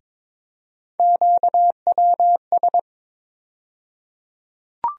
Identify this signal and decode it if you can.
Morse code, heard as a keyed tone, about 21 wpm, 700 Hz. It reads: QWS
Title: USA Call Sign Postfixes - 3 Letters 22wpm